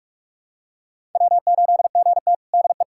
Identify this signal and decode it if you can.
Morse code, heard as a keyed tone, about 46 wpm, 700 Hz. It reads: W8GT DE
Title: USA Callsign DE Callsign - 4 Characters - 45wpm